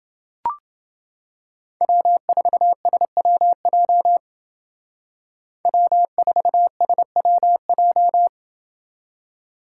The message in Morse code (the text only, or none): W4SWJ W4SWJ